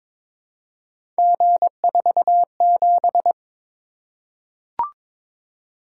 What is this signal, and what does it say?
Morse code, heard as a keyed tone, about 22 wpm, 700 Hz. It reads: G47
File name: practice / 22wpm / Three Letter-Number Combinations - 22wpm